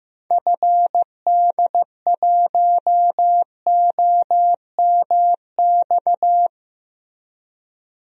Morse code, heard as a keyed tone, about 15 wpm, 700 Hz. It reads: FD1OMX
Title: DXCC-100 Call Signs - 6 Characters - Repeated 2x - 15wpm